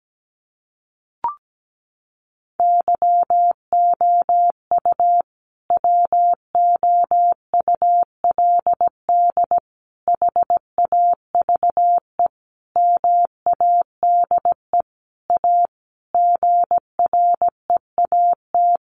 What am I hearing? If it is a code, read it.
Morse code, heard as a keyed tone, about 17 wpm, 700 Hz. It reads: YOU WOULD HAVE MADE A GREAT
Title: Sentences from Top 1500 Words 17wpm